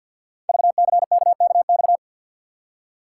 Morse code, heard as a keyed tone, about 52 wpm, 700 Hz. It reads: VCKKX
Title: Code Groups - 5 Letters - 50wpm